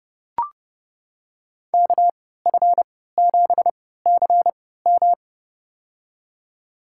Morse code, heard as a keyed tone, about 31 wpm, spaced as for 17 wpm, 700 Hz. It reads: KF7CM
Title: USA Call Signs - 5 Characters - 3x Character Spacing - 30wpm